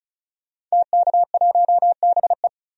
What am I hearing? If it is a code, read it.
Morse code, heard as a keyed tone, about 34 wpm, 700 Hz. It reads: TK1BE
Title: DXCC-100 Call Signs - 5 Characters - 35wpm